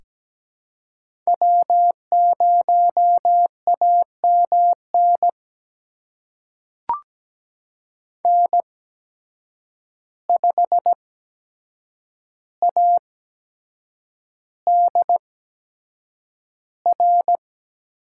Morse code, heard as a keyed tone, about 17 wpm, spaced as for 7 wpm, 700 Hz. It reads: W0AMN N5ADR
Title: USA Call Signs - 8x Character Spacing - 17wpm